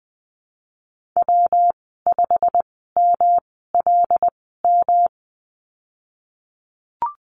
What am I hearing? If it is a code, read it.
Morse code, heard as a keyed tone, about 19 wpm, spaced as for 14 wpm, 700 Hz. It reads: W5MLM
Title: 219 Lesson 35g - Number 0 - Call Signs 20wpm